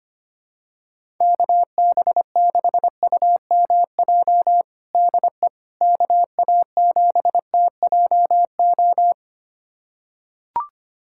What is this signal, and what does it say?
Morse code, heard as a keyed tone, about 25 wpm, 700 Hz. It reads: KB6UMJ DE KA7TJO
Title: USA Callsign DE Callsign - 6 Characters - 25wpm